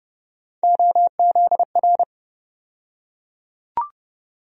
Morse code, heard as a keyed tone, about 30 wpm, 700 Hz. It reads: OZR
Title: USA Call Sign Postfixes - 3 Letters 30wpm